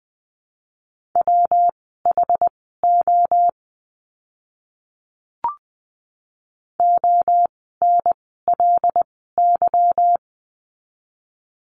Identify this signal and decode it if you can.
Morse code, heard as a keyed tone, about 19 wpm, spaced as for 14 wpm, 700 Hz. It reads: WHO ONLY
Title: 160 Lesson 26f - Letter g - Words 20wpm